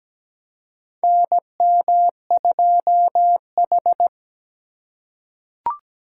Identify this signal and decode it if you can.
Morse code, heard as a keyed tone, about 17 wpm, 700 Hz. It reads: NM2H
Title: USA Call Signs - 4 Characters - 4x Character Spacing - 17wpm